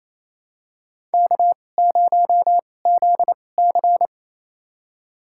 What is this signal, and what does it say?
Morse code, heard as a keyed tone, about 28 wpm, spaced as for 20 wpm, 700 Hz. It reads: K0ZC
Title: USA Call Signs - 4 Characters - 2x Character Spacing - 28wpm